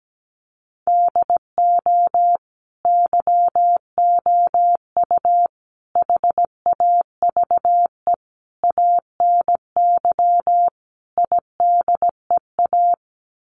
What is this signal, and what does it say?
Morse code, heard as a keyed tone, about 17 wpm, 700 Hz. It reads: DO YOU HAVE ANY IDEA
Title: Sets of 5 Words Encore 17wpm